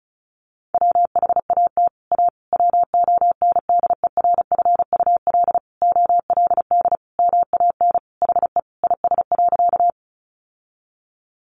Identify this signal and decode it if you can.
Morse code, heard as a keyed tone, about 34 wpm, 700 Hz. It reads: WHAT A WONDERFUL OLD MAN HE IS.
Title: Sentences from Top 1100 Words 35wpm